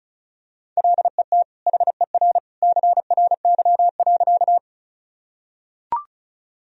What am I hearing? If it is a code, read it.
Morse code, heard as a keyed tone, about 34 wpm, 700 Hz. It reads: LET HER CRY.
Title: Sentences from Top 1000 Words - Review Part 2 - 35wpm